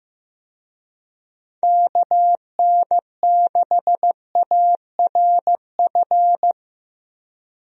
Morse code, heard as a keyed tone, about 15 wpm, 700 Hz. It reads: KN6ARF
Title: USA Call Signs - 6 Characters - Repeated 3x - 15wpm